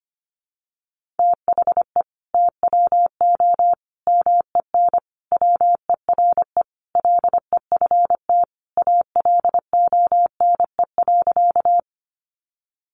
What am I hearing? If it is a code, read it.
Morse code, heard as a keyed tone, about 25 wpm, 700 Hz. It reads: THE TWO MEN WERE LEFT ALONE.
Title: Sentences from Top 1000 Words 25wpm